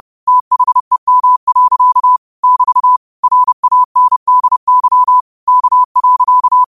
Morse code, heard as a keyed tone, about 30 wpm, 1000 Hz. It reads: THEM. <BT> RANDY K.
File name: qotdfiles / 30wpm